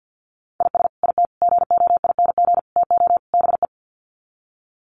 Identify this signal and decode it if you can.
Morse code, heard as a keyed tone, about 48 wpm, 700 Hz. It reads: IS IT GOING TO BE
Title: Sets of 5 Words Encore2 50wpm